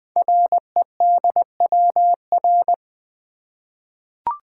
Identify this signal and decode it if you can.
Morse code, heard as a keyed tone, about 20 wpm, 700 Hz. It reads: REDWR
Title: Code Groups - 5 Letters - 20wpm